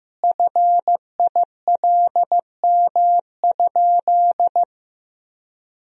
Morse code, heard as a keyed tone, about 15 wpm, 700 Hz. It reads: FILM?
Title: Sentences from Top 1000 Words - Review Part 3 - 15wpm